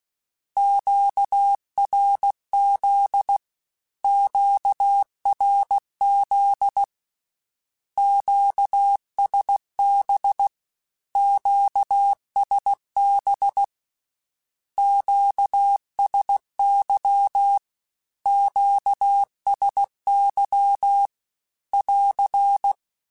Morse code, heard as a keyed tone, about 16 wpm, 800 Hz. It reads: QRZ QRZ QSB QSB QSY QSY <AR>
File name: Q Calls / Q Calls-mp3s / Q Calls Twice @15